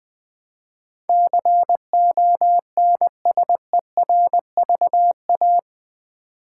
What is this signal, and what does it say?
Morse code, heard as a keyed tone, about 20 wpm, 700 Hz. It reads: CONSERVA
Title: Build A Word - 4 Letters At A Time - 12-Letter Words - 20wpm